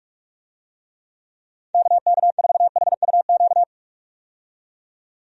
Morse code, heard as a keyed tone, about 42 wpm, 700 Hz. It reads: KK4HUQ
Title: USA Call Signs - 6 Characters - 45wpm